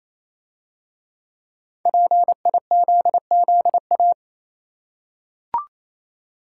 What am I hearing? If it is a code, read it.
Morse code, heard as a keyed tone, about 27 wpm, 700 Hz. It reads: PIZZA
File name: Easy Words - Speed-Racing with Hints - 22wpm